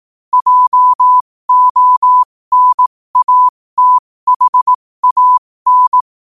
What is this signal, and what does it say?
Morse code, heard as a keyed tone, about 18 wpm, spaced as for 15 wpm, 1000 Hz. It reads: JONATHAN
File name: qotdfiles / 15wpm